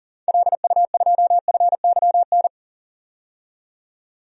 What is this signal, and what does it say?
Morse code, heard as a keyed tone, about 40 wpm, 700 Hz. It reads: LU2FYN
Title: DXCC-100 Call Signs - 6 Characters - 40wpm